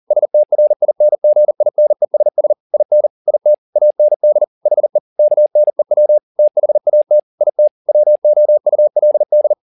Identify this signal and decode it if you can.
Morse code, heard as a keyed tone, about 40 wpm, 600 Hz. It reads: STRINGINESS IN IT AND HE KNEW THAT IT WOULD
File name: the old man and the sea 2 40wpm